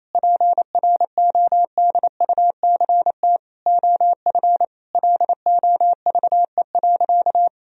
Morse code, heard as a keyed tone, about 28 wpm, 700 Hz. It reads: PRODUCT OF LOVE.
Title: Sentences from Top 1000 Words - Review Part 2 - 28wpm